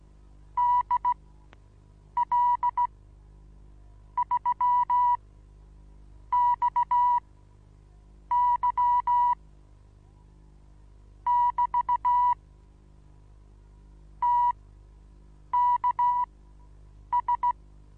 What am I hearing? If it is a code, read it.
Morse code, heard as a keyed tone, about 16 wpm, spaced as for 6 wpm, 980 Hz. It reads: DL3XY <BT> TKS
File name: Onlinelehrgang / morsen / DJ4UF Morsen25